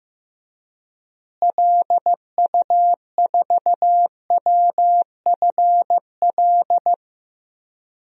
Morse code, heard as a keyed tone, about 16 wpm, 700 Hz. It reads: LU4WFL
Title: DXCC-100 Call Signs - 6 Characters - Repeated 2x - 15wpm